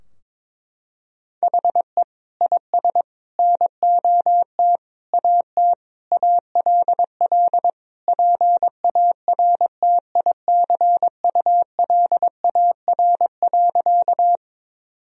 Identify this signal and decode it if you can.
Morse code, heard as a keyed tone, about 21 wpm, 700 Hz. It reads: HE IS NOT AT ALL PARTICULAR.
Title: Sentences from Top 1000 Words - Review Part 4 - 22wpm